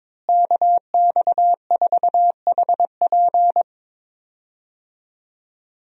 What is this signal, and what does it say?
Morse code, heard as a keyed tone, about 22 wpm, 700 Hz. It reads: KX4HP
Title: USA Call Signs - 5 Characters - Repeated 3x - 22wpm